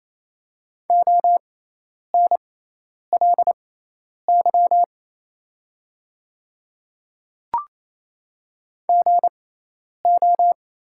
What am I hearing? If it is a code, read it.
Morse code, heard as a keyed tone, about 28 wpm, spaced as for 10 wpm, 700 Hz. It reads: ONLY GO
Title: 6x Character Spacing - Top 200 Words - 28wpm